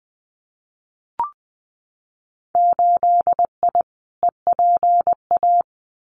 Morse code, heard as a keyed tone, about 20 wpm, 700 Hz. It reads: 8I EPA
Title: Winter Field Day Exchanges Encore 20wpm